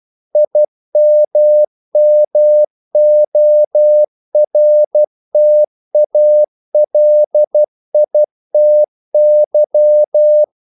Morse code, heard as a keyed tone, about 12 wpm, 600 Hz. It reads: IMMORTALITY
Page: English quotations in CW (telegraphy - morse)